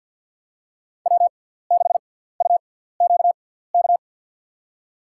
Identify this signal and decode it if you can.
Morse code, heard as a keyed tone, about 53 wpm, spaced as for 17 wpm, 700 Hz. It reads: W6UQK
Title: USA Call Signs - 5 Characters - 6x Character Spacing - 50wpm